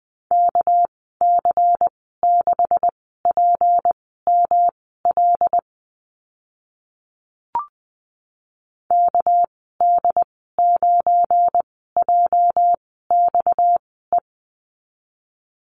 Morse code, heard as a keyed tone, about 20 wpm, spaced as for 14 wpm, 700 Hz. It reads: KC6PML KD9JXE
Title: 238 Lesson 38g - Letter q - Call Signs 20wpm